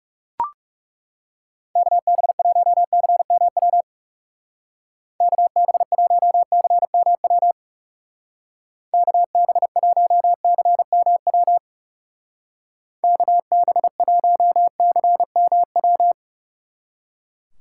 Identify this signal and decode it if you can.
Morse code, heard as a keyed tone, about 36 wpm, 690 Hz. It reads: KB1CMW KB1CMW KB1CMW KB1CMW
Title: Speed Racing - USA Call Signs 30wpm